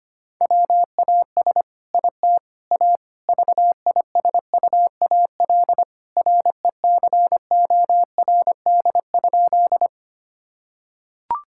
Morse code, heard as a keyed tone, about 25 wpm, 700 Hz. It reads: WAS IT A VISUAL RECORD?
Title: Sentences from Top 1000 Words - Review Part 6 - 25wpm